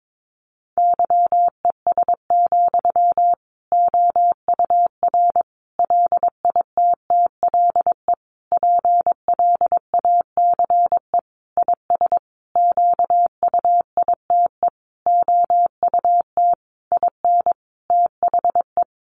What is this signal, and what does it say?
Morse code, heard as a keyed tone, about 22 wpm, 700 Hz. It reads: YES, OUR LITTLE PLACE IS QUITE OUT IN THE
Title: Sentences from Top 700 Words 22wpm